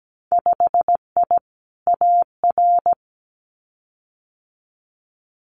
Morse code, heard as a keyed tone, about 17 wpm, 700 Hz. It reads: 5I AR
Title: Winter Field Day Exchanges Encore 17wpm